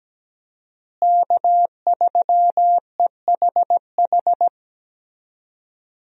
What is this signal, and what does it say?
Morse code, heard as a keyed tone, about 17 wpm, 700 Hz. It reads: K3EHH
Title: USA Call Signs - 5 Characters - Repeated 3x - 17wpm